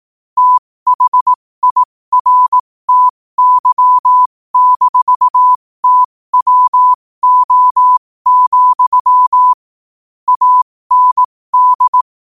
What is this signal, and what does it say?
Morse code, heard as a keyed tone, about 18 wpm, spaced as for 15 wpm, 1000 Hz. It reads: THIRTY-TWO, AND
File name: qotdfiles / 15wpm